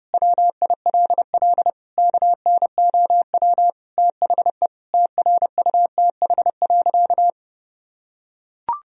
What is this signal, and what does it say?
Morse code, heard as a keyed tone, about 30 wpm, 700 Hz. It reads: WILL KNOW THE TRUTH.
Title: Sentences from Top 1000 Words - Review Part 3 - 30wpm